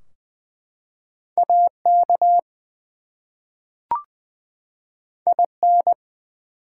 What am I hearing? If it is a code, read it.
Morse code, heard as a keyed tone, about 20 wpm, 700 Hz. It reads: AK IN
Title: US States and Canadian Provinces 20wpm